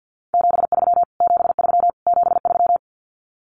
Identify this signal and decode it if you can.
Morse code, heard as a keyed tone, about 50 wpm, 700 Hz. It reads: 73 73 73
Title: Abbreviations Repeated 3x - 50wpm